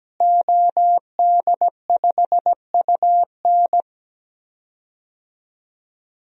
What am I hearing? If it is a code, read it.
Morse code, heard as a keyed tone, about 17 wpm, 700 Hz. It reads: OD5UN
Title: DXCC-100 Call Signs - 5 Characters - Repeated 3x - 17wpm